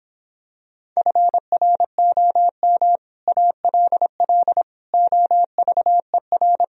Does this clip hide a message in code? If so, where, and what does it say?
Yes: Morse code from 1.0 to 6.6 s, heard as a keyed tone, about 26 wpm, 700 Hz. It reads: FROM ALL OVER